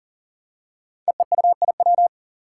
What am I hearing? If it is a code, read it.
Morse code, heard as a keyed tone, about 40 wpm, 700 Hz. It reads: EEUIW
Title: Five Challenging Letter-Number Combinations 40wpm